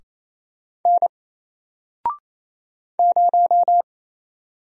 Morse code, heard as a keyed tone, about 27 wpm, 700 Hz. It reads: N 0
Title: Single Letter-Number Alt 28wpm